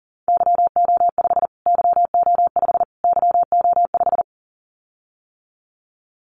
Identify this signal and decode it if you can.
Morse code, heard as a keyed tone, about 40 wpm, 700 Hz. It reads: YO5 YO5 YO5
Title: Callsign Prefixes for Top 25 DXCC - Repeated 3x - 40wpm